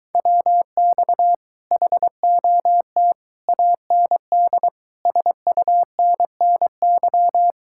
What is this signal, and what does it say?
Morse code, heard as a keyed tone, about 23 wpm, 700 Hz. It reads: WX HOT AND SUNNY